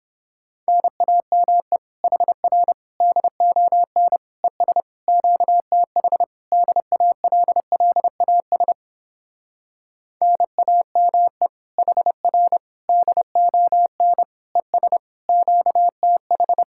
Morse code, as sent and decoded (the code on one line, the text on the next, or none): -. .- -- . / .... .-. / -.. --- -. / . ... / --.- - .... / -.. .- .-.. .-.. .- ... / -. .- -- . / .... .-. / -.. --- -. / . ... / --.- - ....
NAME HR DON ES QTH DALLAS NAME HR DON ES QTH